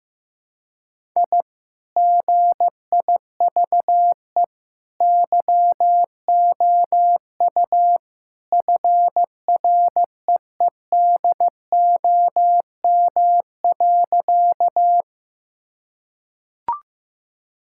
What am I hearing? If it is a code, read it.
Morse code, heard as a keyed tone, about 15 wpm, 700 Hz. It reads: I GIVE YOU FREEDOM.
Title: Sentences from Top 1000 Words - Review Part 5 - 15wpm